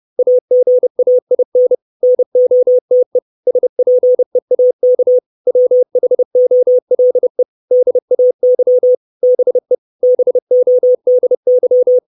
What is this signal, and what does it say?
Morse code, heard as a keyed tone, about 30 wpm, 500 Hz. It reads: AGAIN NOTE SPEAK WHOLE DAY BE BODY
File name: 632 Words 30WPM